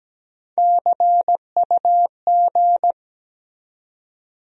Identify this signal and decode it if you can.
Morse code, heard as a keyed tone, about 17 wpm, 700 Hz. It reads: CUG
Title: USA Call Sign Postfixes - 3 Letters 17wpm